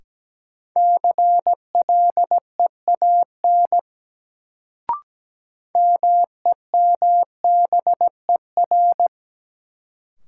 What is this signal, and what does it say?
Morse code, heard as a keyed tone, about 17 wpm, 700 Hz. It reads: CLEAN MEMBER
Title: Sentences from Top 1100 Words - New Word Drill - 17wpm